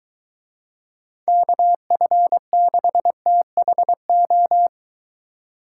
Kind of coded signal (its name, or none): Morse code